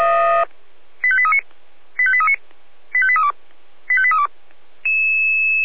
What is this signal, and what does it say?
Five-tone selective call (ZVEI), heard as quick runs of five tones, tones of about 70 ms. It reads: 85839 85839 85832 85832
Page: SDR/ZVEI